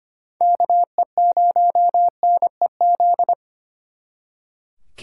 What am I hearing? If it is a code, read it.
Morse code, heard as a keyed tone, about 32 wpm, 700 Hz. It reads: KE0NEZ